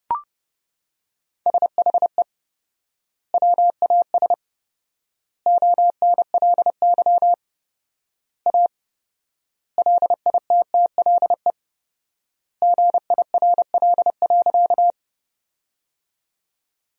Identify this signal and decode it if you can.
Morse code, heard as a keyed tone, about 30 wpm, 700 Hz. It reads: SHE WAS ONLY A LITTLE GIRL.
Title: 4x Word-Spacing - Sentences from Top 400 Words - 30wpm